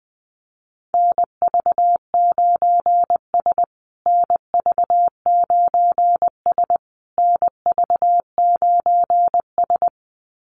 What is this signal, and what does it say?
Morse code, heard as a keyed tone, about 20 wpm, 700 Hz. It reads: NV9S NV9S NV9S
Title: USA Call Signs Repeated 3x 20wpm